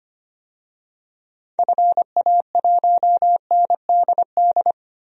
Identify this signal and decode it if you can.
Morse code, heard as a keyed tone, about 25 wpm, 700 Hz. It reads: FA1NDD
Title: DXCC-100 Call Signs - 6 Characters - 25wpm